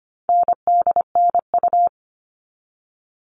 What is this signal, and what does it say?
Morse code, heard as a keyed tone, about 25 wpm, 700 Hz. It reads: NDNU
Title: All Cut Numbers - 4-Digit 25wpm